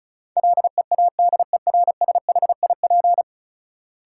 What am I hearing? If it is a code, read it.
Morse code, heard as a keyed tone, about 36 wpm, 700 Hz. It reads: LEADERSHIP
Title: Sentences from Top 1600 Words - New Word Drill Alt - 35wpm